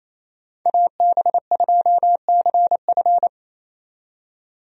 Morse code, heard as a keyed tone, about 28 wpm, 700 Hz. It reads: AB2CF
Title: USA Call Signs - 5 Characters - 28wpm